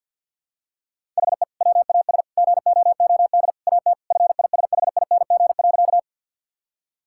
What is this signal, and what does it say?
Morse code, heard as a keyed tone, about 50 wpm, 700 Hz. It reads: HE WAS GOOD AT FISHING.
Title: Sentences from Top 1000 Words - Review Part 5 - 50wpm